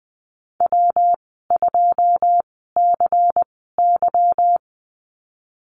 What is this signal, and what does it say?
Morse code, heard as a keyed tone, about 19 wpm, spaced as for 14 wpm, 700 Hz. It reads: W2CY
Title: 167 Lesson 27g - Number 7 - Call Signs 20wpm